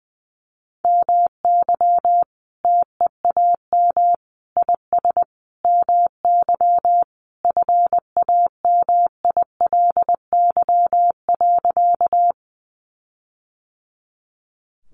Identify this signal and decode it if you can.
Morse code, heard as a keyed tone, about 20 wpm, 700 Hz. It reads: MY TEAM IS MY FAMILY.
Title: Sentences from Top 1500 Words 20wpm